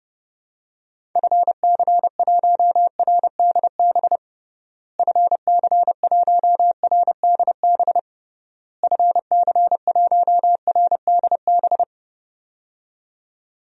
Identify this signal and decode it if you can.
Morse code, heard as a keyed tone, about 30 wpm, 700 Hz. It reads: FC1RDB FC1RDB FC1RDB
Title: DXCC-100 Call Signs - 6 Characters - Repeated 3x - 30wpm